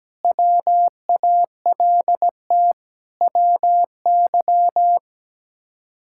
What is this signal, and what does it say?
Morse code, heard as a keyed tone, about 17 wpm, 700 Hz. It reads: WALT WY
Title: K1USN SST QSO Practice - 17wpm